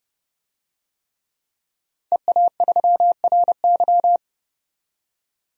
Morse code, heard as a keyed tone, about 32 wpm, 700 Hz. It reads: EA3RY
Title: DXCC-100 Call Signs - 5 Characters - 30wpm